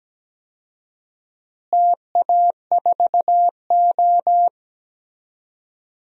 Morse code, heard as a keyed tone, about 17 wpm, 700 Hz. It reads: TA4O
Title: DXCC-100 Call Signs - 4 Characters - Repeated 3x - 17wpm